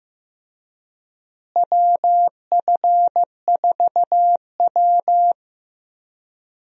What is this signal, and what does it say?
Morse code, heard as a keyed tone, about 15 wpm, 700 Hz. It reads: WF4W
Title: USA Call Signs - 4 Characters - Repeated 2x - 15wpm